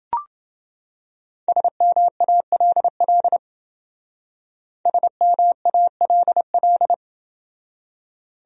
Morse code, heard as a keyed tone, about 28 wpm, 700 Hz. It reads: SMALL SMALL